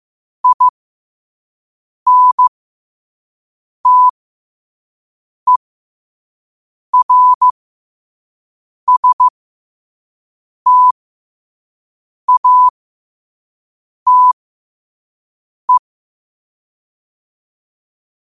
Morse code, heard as a keyed tone, about 15 wpm, spaced as for 5 wpm, 1000 Hz. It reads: INTERSTATE
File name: qotdfiles / 5wpm